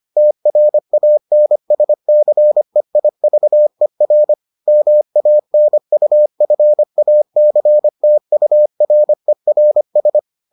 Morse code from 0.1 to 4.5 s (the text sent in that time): TRANSCEIVER